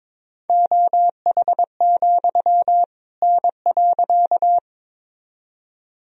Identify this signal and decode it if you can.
Morse code, heard as a keyed tone, about 22 wpm, 700 Hz. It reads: OH, N.
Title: Sentences from Top 500 Words and Abbreviations 22wpm